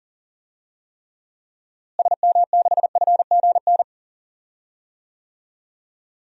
Morse code, heard as a keyed tone, about 40 wpm, 700 Hz. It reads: SM6FGN
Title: DXCC-100 Call Signs - 6 Characters - Repeated 3x - 40wpm